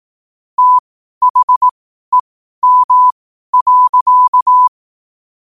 Morse code, heard as a keyed tone, about 18 wpm, spaced as for 12 wpm, 1000 Hz. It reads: THEM.
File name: qotdfiles / 12wpm